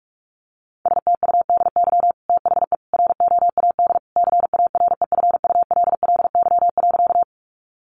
Morse code, heard as a keyed tone, about 44 wpm, 700 Hz. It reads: STUDY THE ROAD CAREFULLY.
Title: Sentences from Top 1300 Words 45wpm